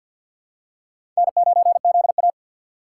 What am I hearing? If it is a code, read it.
Morse code, heard as a keyed tone, about 51 wpm, 700 Hz. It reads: N9ZA
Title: USA Call Signs - 4 Characters - 2x Character Spacing - 50wpm